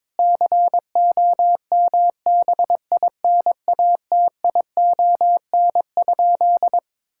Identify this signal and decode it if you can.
Morse code, heard as a keyed tone, about 22 wpm, 700 Hz. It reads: COMBINATION?
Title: Sentences from Top 1000 Words - Review Part 6 - 22wpm